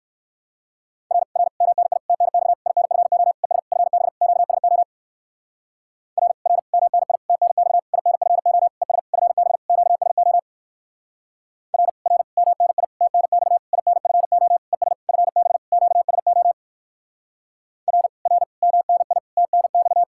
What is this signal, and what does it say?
Morse code, heard as a keyed tone, about 57 wpm, 700 Hz. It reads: R R MNI TNX INFO ES FB QSO R R MNI TNX INFO ES FB QSO R R MNI TNX INFO ES FB QSO R R MNI TNX